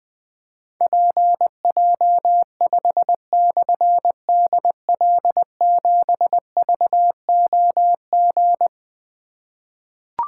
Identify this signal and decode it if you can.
Morse code, heard as a keyed tone, about 20 wpm, 700 Hz. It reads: PJ5/DL7VOG
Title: Difficult Call Signs 20wpm